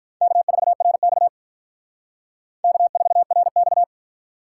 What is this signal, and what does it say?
Morse code, heard as a keyed tone, about 44 wpm, 700 Hz. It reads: K4RX K4RX